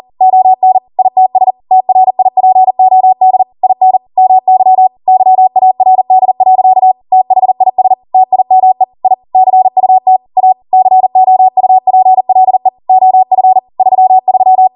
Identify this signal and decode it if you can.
Morse code, heard as a keyed tone, about 40 wpm, 750 Hz. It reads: ON ITS TRIPOD IN MY YARD. THIS TIME I CUT A COUPLE OF 33